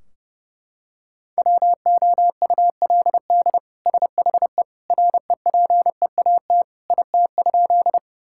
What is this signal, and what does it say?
Morse code, heard as a keyed tone, about 30 wpm, 700 Hz. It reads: WOULD SHE REPEAT IT?
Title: Sentences from Top 1600 Words 30wpm